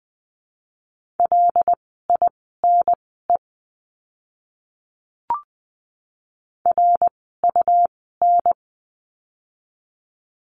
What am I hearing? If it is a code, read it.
Morse code, heard as a keyed tone, about 20 wpm, spaced as for 14 wpm, 700 Hz. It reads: LINE RUN
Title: 115 Lesson 19f - Letter w - Words 20wpm